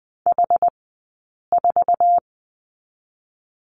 Morse code, heard as a keyed tone, about 20 wpm, spaced as for 8 wpm, 700 Hz. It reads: H4
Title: 086 Lesson 15d - Number 5 - Two Characters 20wpm